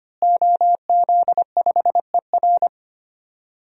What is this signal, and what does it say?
Morse code, heard as a keyed tone, about 25 wpm, 700 Hz. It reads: OZ5ER
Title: DXCC-100 Call Signs - 5 Characters - Repeated 3x - 25wpm